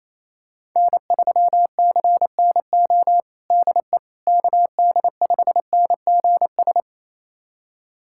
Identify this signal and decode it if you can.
Morse code, heard as a keyed tone, about 28 wpm, 700 Hz. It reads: N3CNO DE KD5NGS
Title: USA Callsign DE Callsign 28wpm